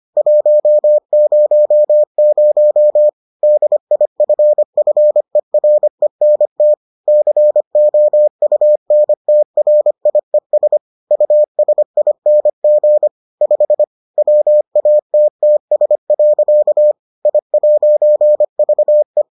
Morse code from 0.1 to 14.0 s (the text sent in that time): 100 DIFFERENT COUNTRIES USING 5